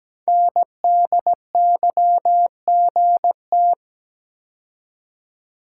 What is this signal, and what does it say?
Morse code, heard as a keyed tone, about 17 wpm, 700 Hz. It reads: NDYGT
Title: Code Groups - 5 Letter:Numbers - 17wpm